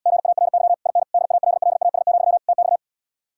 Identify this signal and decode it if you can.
Morse code, heard as a keyed tone, about 78 wpm, 700 Hz. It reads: GARY IN DALLAS, TX